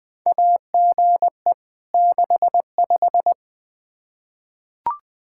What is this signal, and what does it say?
Morse code, heard as a keyed tone, about 20 wpm, 700 Hz. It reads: AGE 65
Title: K1USN SST QSO Practice 20wpm